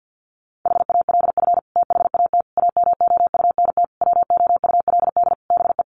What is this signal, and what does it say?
Morse code, heard as a keyed tone, about 50 wpm, 700 Hz. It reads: HALF THAT AMOUNT WOULD BE
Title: Sentences from Top 1400 Words 50wpm